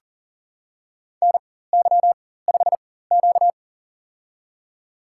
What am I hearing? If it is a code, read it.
Morse code, heard as a keyed tone, about 37 wpm, spaced as for 18 wpm, 700 Hz. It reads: NY5Q
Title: USA Call Signs - 4 Characters - 4x Character Spacing - 40wpm